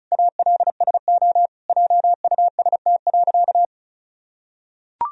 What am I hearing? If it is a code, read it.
Morse code, heard as a keyed tone, about 34 wpm, 700 Hz. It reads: ALSO JUST.
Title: Sentences from Top 1000 Words - Review Part 1 - 35wpm